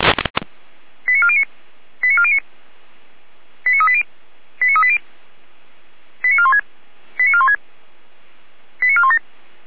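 Five-tone selective call (ZVEI), heard as quick runs of five tones, tones of about 70 ms. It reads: 89409 89409 89380 89380 89416 89416 89417
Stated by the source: ZVEI five-tone selective call received on an SDR.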